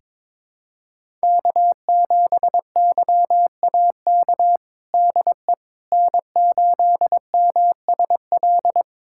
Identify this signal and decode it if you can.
Morse code, heard as a keyed tone, about 22 wpm, 700 Hz. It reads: K7YAK DE N8MSL
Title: USA Callsign DE Callsign - 5 Characters - 22wpm